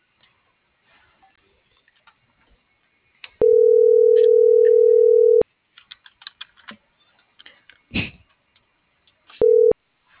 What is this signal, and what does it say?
Ringback tone (North America), 440 + 480 Hz.